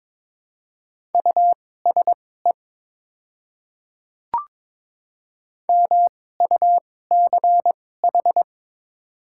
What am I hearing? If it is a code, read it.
Morse code, heard as a keyed tone, about 22 wpm, spaced as for 16 wpm, 700 Hz. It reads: USE MUCH
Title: 2x Character Spacing - Top 300 Words - 22wpm